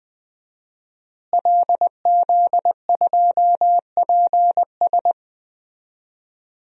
Morse code, heard as a keyed tone, about 20 wpm, 700 Hz. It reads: LZ2PS
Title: DXCC-100 Call Signs - 5 Characters - Repeated 2x - 20wpm